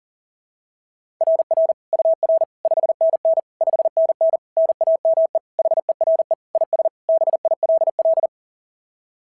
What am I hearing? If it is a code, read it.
Morse code, heard as a keyed tone, about 37 wpm, 660 Hz. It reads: RR UR 5NN 5NN NAME HERE IS BILL
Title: SOTA QSO Practice 40wpm